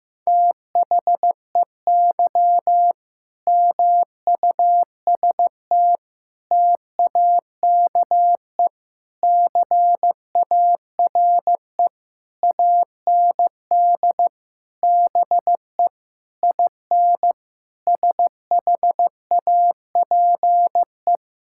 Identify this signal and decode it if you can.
Morse code, heard as a keyed tone, about 15 wpm, 700 Hz. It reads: THEY MUST TAKE CARE AND BE IN SHAPE
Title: Sentences from Top 1000 Words - Review Part 6 - 15wpm